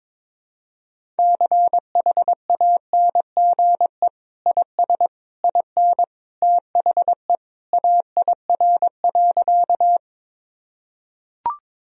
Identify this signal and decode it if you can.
Morse code, heard as a keyed tone, about 22 wpm, 700 Hz. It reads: CHANGE IS IN THE AIR.
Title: Sentences from Top 1000 Words - Review Part 1 - 22wpm